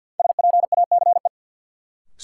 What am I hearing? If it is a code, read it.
Morse code, heard as a keyed tone, about 47 wpm, 700 Hz. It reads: SPACE
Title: Sentences from Top 1500 Words - New Word Drill Alt - 50wpm